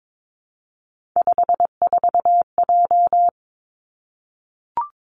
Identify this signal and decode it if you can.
Morse code, heard as a keyed tone, about 22 wpm, 700 Hz. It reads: 54J
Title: Three Challenging Letter-Number Combinations 22wpm